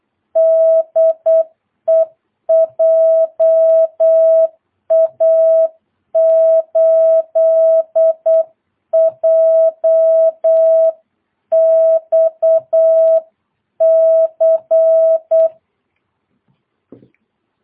Morse code, heard as a keyed tone, about 8 wpm, 660 Hz. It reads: DE JA8JXC
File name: jh8cbh / 2022cw